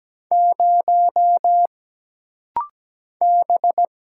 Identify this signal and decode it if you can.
Morse code, heard as a keyed tone, about 17 wpm, 700 Hz. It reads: 0 B
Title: Single Letter-Number Alt - Rapid-Fire 17wpm